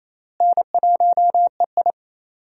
Morse code, heard as a keyed tone, about 27 wpm, 700 Hz. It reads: N1EI
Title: USA Call Signs - 4 Characters - Repeated 2x - 28wpm